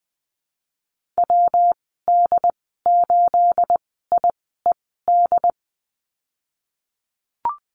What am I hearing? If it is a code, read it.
Morse code, heard as a keyed tone, about 20 wpm, 700 Hz. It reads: W D 8 I E D
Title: 238 Lesson 38g - Letter q - Call Signs 20wpm